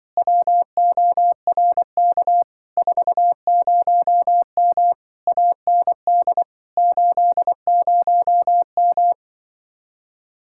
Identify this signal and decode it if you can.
Morse code, heard as a keyed tone, about 24 wpm, 700 Hz. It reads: WORK 40M AND 80M
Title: Speed-Racing Common QSO Phrases - 20wpm